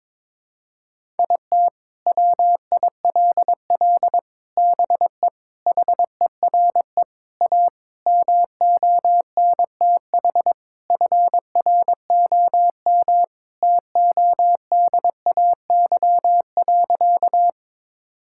Morse code, heard as a keyed tone, about 22 wpm, 700 Hz. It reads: IT WILL BE HERE A MONTH FROM TODAY.
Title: Sentences from Top 1000 Words - Review Part 5 - 22wpm